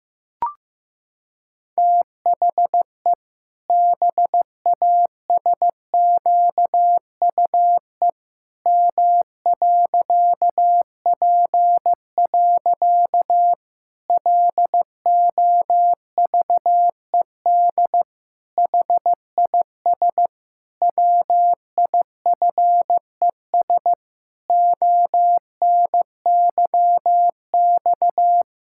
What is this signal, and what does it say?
Morse code, heard as a keyed tone, about 15 wpm, 700 Hz. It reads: THE BASQUE M.P. LOVED HIS WIFES ONYX
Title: Pangram Challenge v2 - 15wpm